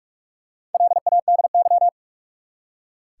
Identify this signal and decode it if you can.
Morse code, heard as a keyed tone, about 46 wpm, 700 Hz. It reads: LADY
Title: Four Letter Words - 45wpm